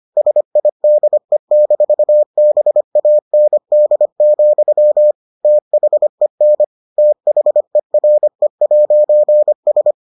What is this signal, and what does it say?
Morse code, heard as a keyed tone, about 25 wpm, 600 Hz. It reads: SIDE-BAND, THEN THERE'S